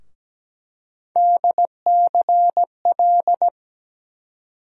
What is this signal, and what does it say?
Morse code, heard as a keyed tone, about 17 wpm, 700 Hz. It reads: DCL
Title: USA Call Sign Postfixes - 3 Letters 17wpm